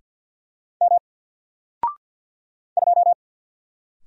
Morse code, heard as a keyed tone, about 50 wpm, 700 Hz. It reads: M 2
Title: Single Letter-Number 50wpm